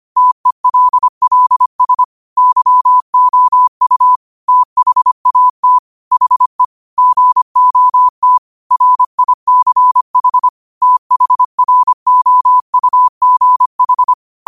Morse code, heard as a keyed tone, about 25 wpm, 1000 Hz. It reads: TELLS YOU THAT HE GOT RICH THROUGH